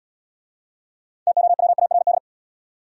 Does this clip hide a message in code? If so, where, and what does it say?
Yes: Morse code from 1.3 to 2.2 s, heard as a keyed tone, about 77 wpm, 700 Hz. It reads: TOWARD